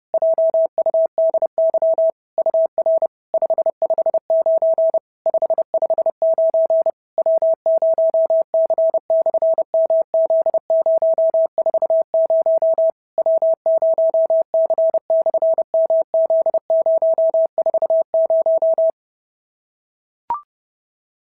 Morse code, heard as a keyed tone, about 30 wpm, 650 Hz. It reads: JUDY UR 559 559 W0C/MZ040 W0C/MZ040
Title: SOTA QSO Practice 30wpm